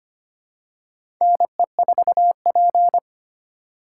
Morse code, heard as a keyed tone, about 25 wpm, 700 Hz. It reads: NE4P
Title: USA Call Signs - 3x Character Spacing - 25wpm